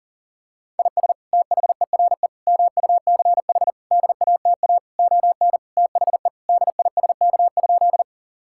Morse code, heard as a keyed tone, about 37 wpm, 700 Hz. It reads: IS THERE MUCH DATA ON THE DISK?
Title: Sentences from Top 1000 Words - Review Part 6 - 40wpm